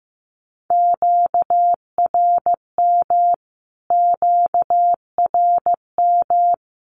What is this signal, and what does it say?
Morse code, heard as a keyed tone, about 15 wpm, 700 Hz. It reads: QRM QRM
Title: Q-Codes Repeated 3x 15wpm